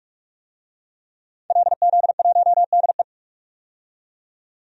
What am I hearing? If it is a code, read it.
Morse code, heard as a keyed tone, about 44 wpm, 690 Hz. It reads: LZ1DE
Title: DXCC-100 Call Signs - 5 Characters - 45wpm